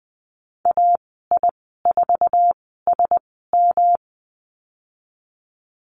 Morse code, heard as a keyed tone, about 20 wpm, spaced as for 14 wpm, 700 Hz. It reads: AI4SM
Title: 161 Lesson 26g - Letter g - Call Signs 20wpm